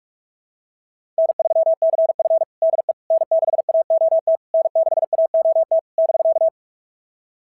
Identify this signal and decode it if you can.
Morse code, heard as a keyed tone, about 46 wpm, 690 Hz. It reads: N3CF DE N6AOT N6AOT <BK>
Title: SOTA QSO Practice 45wpm